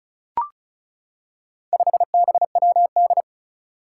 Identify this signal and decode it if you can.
Morse code, heard as a keyed tone, about 36 wpm, 700 Hz. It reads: 5BWD
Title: Four Challenging Letter-Number Combinations 35wpm